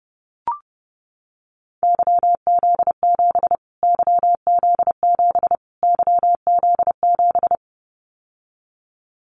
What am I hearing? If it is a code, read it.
Morse code, heard as a keyed tone, about 30 wpm, 700 Hz. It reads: YZ7 YZ7 YZ7
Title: Callsign Prefixes for Top 25 DXCC - Repeated 3x - 30wpm